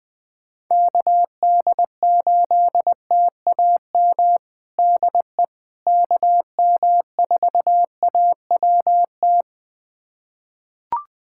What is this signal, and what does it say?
Morse code, heard as a keyed tone, about 20 wpm, 700 Hz. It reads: KD8TAM DE KM4AWT
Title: USA Callsign DE Callsign - 6 Characters - 20wpm